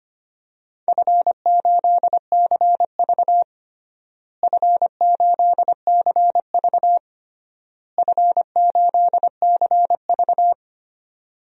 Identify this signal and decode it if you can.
Morse code, heard as a keyed tone, about 25 wpm, 700 Hz. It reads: F8CV F8CV F8CV
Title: DXCC-100 Call Signs - 4 Characters - Repeated 3x - 25wpm